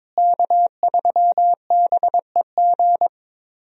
Morse code, heard as a keyed tone, about 22 wpm, 700 Hz. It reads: K3BEG
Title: USA Call Signs - 5 Characters - Repeated 2x - 22wpm